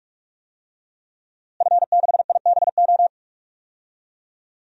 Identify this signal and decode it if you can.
Morse code, heard as a keyed tone, about 46 wpm, 700 Hz. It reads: F6IBO
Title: DXCC-100 Call Signs - 5 Characters - Repeated 2x - 45wpm